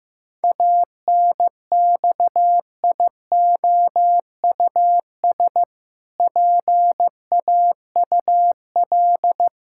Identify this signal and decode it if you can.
Morse code, heard as a keyed tone, about 15 wpm, 700 Hz. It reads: ANXIOUS PAUL
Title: Pangram Challenge v2 - 15wpm